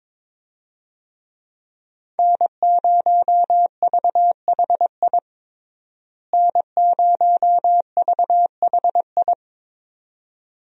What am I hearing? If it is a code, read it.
Morse code, heard as a keyed tone, about 21 wpm, 700 Hz. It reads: N0VHI N0VHI
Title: USA Call Signs - 5 Characters - Repeated 2x - 22wpm